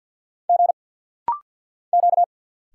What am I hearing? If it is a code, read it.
Morse code, heard as a keyed tone, about 50 wpm, 690 Hz. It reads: G Q
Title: Single Letters - Rapid-Fire 50wpm